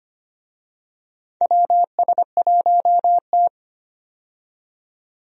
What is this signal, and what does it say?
Morse code, heard as a keyed tone, about 25 wpm, 700 Hz. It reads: WS1T
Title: USA Call Signs - 4 Characters - 25wpm